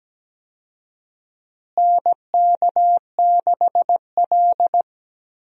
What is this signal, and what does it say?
Morse code, heard as a keyed tone, about 17 wpm, 700 Hz. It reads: NK6L
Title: USA Call Signs - 4 Characters - Repeated 2x - 17wpm